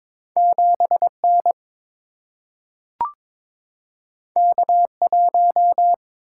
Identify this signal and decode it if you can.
Morse code, heard as a keyed tone, about 22 wpm, 700 Hz. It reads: 7 N K 1
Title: Single-Letter Single-Number 22wpm